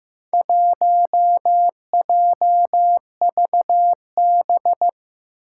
Morse code, heard as a keyed tone, about 15 wpm, 700 Hz. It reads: 1JVB
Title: Four Challenging Letter-Number Combinations 15wpm